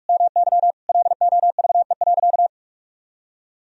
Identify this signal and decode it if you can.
Morse code, heard as a keyed tone, about 47 wpm, 700 Hz. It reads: MY LOVE.
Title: Sentences from Top 400 Words and Abbreviations 45wpm